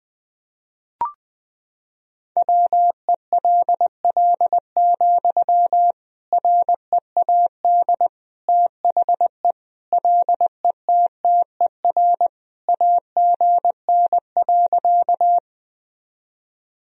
Morse code, heard as a keyed tone, about 21 wpm, 700 Hz. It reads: WELL, READ THE LETTER AGN.
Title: Sentences from Top 300 Words and Abbreviations 20wpm